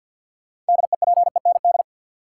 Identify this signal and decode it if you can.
Morse code, heard as a keyed tone, about 51 wpm, 700 Hz. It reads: DEPEND
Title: Sentences from Top 1300 Words - New Word Drill Alt - 50wpm